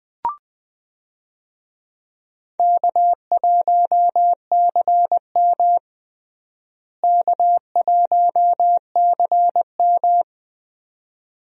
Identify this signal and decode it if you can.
Morse code, heard as a keyed tone, about 20 wpm, 700 Hz. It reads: K1CM K1CM
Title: USA Call Signs - 4 Characters - Repeated 2x - 20wpm